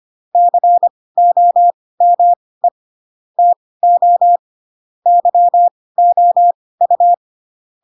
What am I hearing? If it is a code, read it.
Morse code, heard as a keyed tone, about 25 wpm, spaced as for 18 wpm, 700 Hz. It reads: COME TO YOU
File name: Pinocchio / Pinocchio ch2 12 25 ik1hge